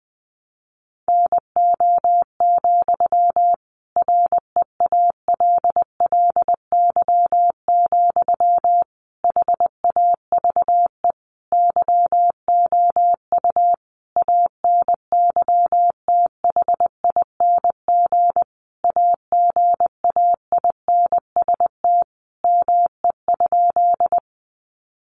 Morse code, heard as a keyed tone, about 20 wpm, 700 Hz. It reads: NO, REALLY, HAVE YOU ANYTHING AGAINST ME?
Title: Sentences from Top 1000 Words 20wpm